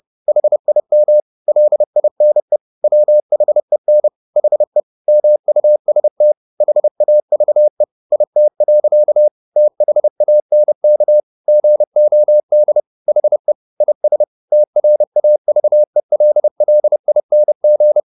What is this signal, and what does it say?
Morse code, heard as a keyed tone, about 30 wpm, 600 Hz. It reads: HIM LINE WHEN HE MUST HAVE IT. THANK GOD HE IS TRAVELLING
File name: the old man and the sea / the old man and the sea 1 30wpm